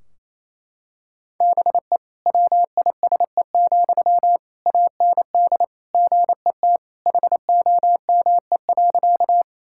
Morse code, heard as a keyed tone, about 28 wpm, 700 Hz. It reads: BE WISE, AND GET HOME.
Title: Sentences from Top 1600 Words 28wpm